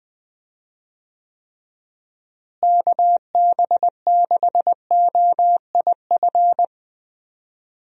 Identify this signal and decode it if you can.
Morse code, heard as a keyed tone, about 20 wpm, 700 Hz. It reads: KB6OIF
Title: USA Call Signs - 6 Characters - Repeated 2x - 20wpm